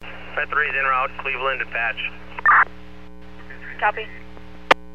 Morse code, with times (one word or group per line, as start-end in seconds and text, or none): none